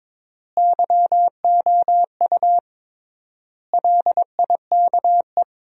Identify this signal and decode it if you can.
Morse code, heard as a keyed tone, about 22 wpm, 700 Hz. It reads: YOU LIKE
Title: 3x Word-Spacing - Sentences from Top 100 Words - 22wpm